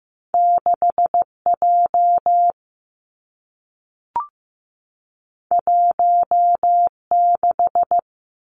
Morse code, heard as a keyed tone, about 15 wpm, 700 Hz. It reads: 6 J 1 6
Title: Two Challenging Letter-Number Combinations 15wpm